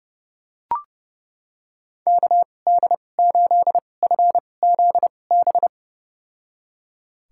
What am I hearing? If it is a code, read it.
Morse code, heard as a keyed tone, about 30 wpm, spaced as for 22 wpm, 700 Hz. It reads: KD8FZB
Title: USA Call Signs - 6 Characters - 2x Character Spacing - 30wpm